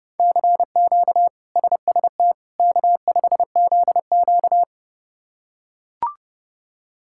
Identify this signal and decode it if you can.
Morse code, heard as a keyed tone, about 32 wpm, 700 Hz. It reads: CQ SST K5ZQ
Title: K1USN SST QSO Practice - 30wpm